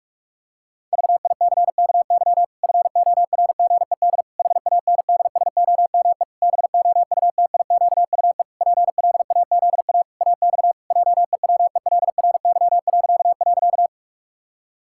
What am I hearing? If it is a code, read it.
Morse code, heard as a keyed tone, about 46 wpm, 700 Hz. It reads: VICKY FORGED HANDSOME BOUTIQUE PLAZA AX JEWELRY..
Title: Pangram Challenge v2 - 45wpm